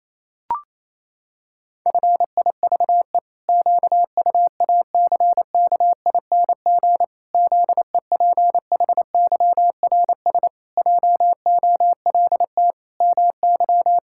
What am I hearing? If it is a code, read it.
Morse code, heard as a keyed tone, about 27 wpm, 700 Hz. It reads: FIVE QUACKING ZEPHYRS JOLT MY
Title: Pangram Challenge v2 - 28wpm